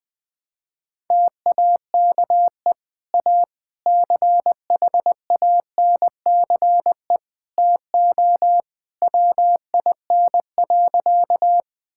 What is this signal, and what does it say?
Morse code, heard as a keyed tone, about 20 wpm, 700 Hz. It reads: TAKE A CHANCE TO WIN.
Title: Sentences with 3-Word Phrases - 20wpm